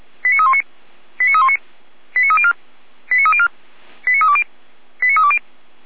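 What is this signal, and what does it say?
Five-tone selective call (ZVEI), heard as quick runs of five tones, tones of about 70 ms. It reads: 89319 89319 89304 89304 89320 89320
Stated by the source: Received by SDR on a ZVEI selective-call channel.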